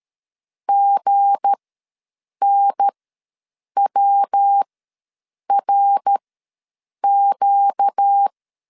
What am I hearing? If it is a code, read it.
Morse code, heard as a keyed tone, about 13 wpm, spaced as for 7 wpm, 800 Hz. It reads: GNWRQ
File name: LEZIONI / 7 GF9Z / 7 GF9 ws 12wpm 3344